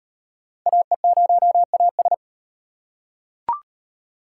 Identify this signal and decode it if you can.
Morse code, heard as a keyed tone, about 38 wpm, 700 Hz. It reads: AE0AS